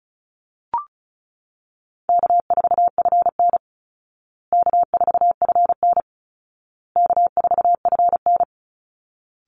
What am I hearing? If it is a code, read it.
Morse code, heard as a keyed tone, about 34 wpm, 700 Hz. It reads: K4FN K4FN K4FN
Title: Easy Call Signs Repeated 3x - 35wpm